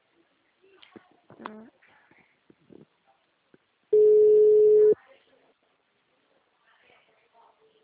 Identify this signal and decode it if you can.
Ringback tone (Europe, 425 Hz).
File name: out- -2030- - - .90742